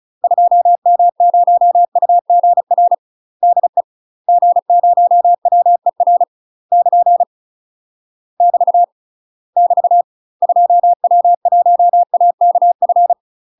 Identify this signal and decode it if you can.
Morse code, heard as a keyed tone, about 35 wpm, 700 Hz. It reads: 2M0UGR DE G0WER <KN> <BT> <BT> 2W1AKF
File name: UK MASTER QSOs TRAINING / 35WPM / UK MASTER QSOs 0042